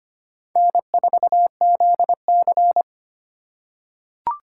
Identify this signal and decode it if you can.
Morse code, heard as a keyed tone, about 25 wpm, 700 Hz. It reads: N4ZC
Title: USA Call Signs - 4 Characters - 6x Character Spacing - 25wpm